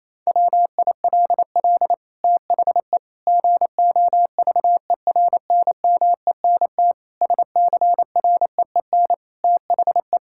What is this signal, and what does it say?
Morse code, heard as a keyed tone, about 27 wpm, 700 Hz. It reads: WILL THE GOVERNMENT SCREEN THE
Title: Sentences from Top 1000 Words - Review Part 6 - 28wpm